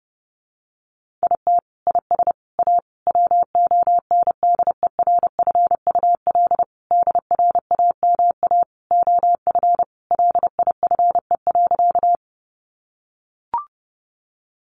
Morse code, heard as a keyed tone, about 30 wpm, 700 Hz. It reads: IT IS A WONDERFUL DRAMA OF LIFE.
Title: Sentences from Top 1500 Words 30wpm